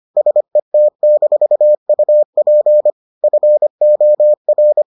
Morse code, heard as a keyed tone, about 25 wpm, 600 Hz. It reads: SET-UP FOR